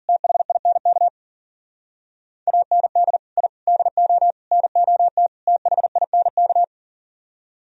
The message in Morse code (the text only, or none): THINK AND I DO NOT THINK